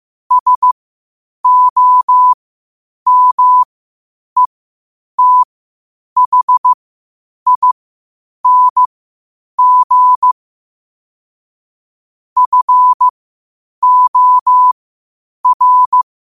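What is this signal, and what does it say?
Morse code, heard as a keyed tone, about 15 wpm, spaced as for 8 wpm, 1000 Hz. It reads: SOMETHING FOR
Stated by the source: Morse code at 8 wpm.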